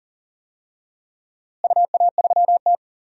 Morse code, heard as a keyed tone, about 40 wpm, 700 Hz. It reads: UA3T
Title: DXCC-100 Call Signs - 4 Characters - Repeated 2x - 40wpm